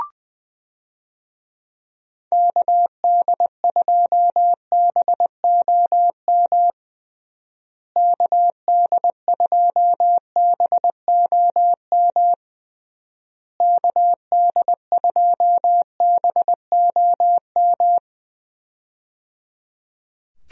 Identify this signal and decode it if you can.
Morse code, heard as a keyed tone, about 21 wpm, 700 Hz. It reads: KD2BOM KD2BOM KD2BOM
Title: USA Call Signs - 6 Characters - Repeated 3x - 20wpm